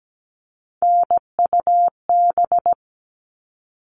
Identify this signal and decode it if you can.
Morse code, heard as a keyed tone, about 17 wpm, 700 Hz. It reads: NUB
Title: All Cut Numbers - 3-Digit 17wpm